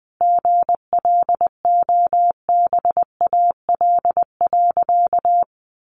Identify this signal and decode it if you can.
Morse code, heard as a keyed tone, about 19 wpm, 700 Hz. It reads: GLOBAL.
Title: Sentences from Top 1200 Words 20wpm